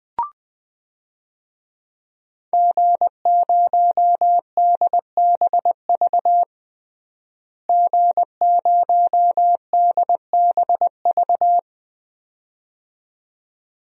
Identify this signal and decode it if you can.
Morse code, heard as a keyed tone, about 20 wpm, 700 Hz. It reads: G0DBV G0DBV
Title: DXCC-100 Call Signs - 5 Characters - Repeated 2x - 20wpm